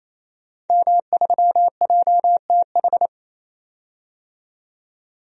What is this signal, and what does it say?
Morse code, heard as a keyed tone, about 28 wpm, 700 Hz. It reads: M3JTH
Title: DXCC-100 Call Signs - 5 Characters - 28wpm